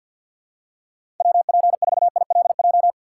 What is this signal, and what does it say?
Morse code, heard as a keyed tone, about 53 wpm, 700 Hz. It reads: W P 4 I L J
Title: USA Call Signs - 6x Character Spacing - 50wpm